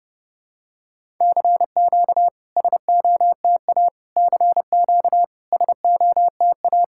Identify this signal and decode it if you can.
Morse code, heard as a keyed tone, about 30 wpm, 690 Hz. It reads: CQ SOTA CQ SOTA
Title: SOTA QSO Practice 30wpm